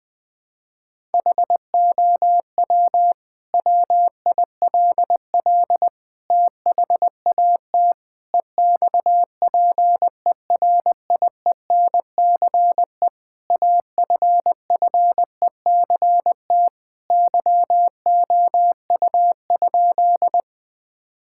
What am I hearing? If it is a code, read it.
Morse code, heard as a keyed tone, about 20 wpm, 700 Hz. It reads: HOW WILL THAT EXPERIENCE AFFECT YOU?
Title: Sentences from Top 1000 Words - Review Part 6 - 20wpm